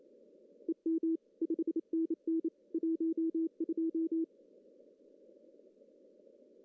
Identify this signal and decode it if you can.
Morse code, heard as a keyed tone, about 28 wpm, 340 Hz. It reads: EM 5NN 12
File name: CW / cqww-cw-k5tr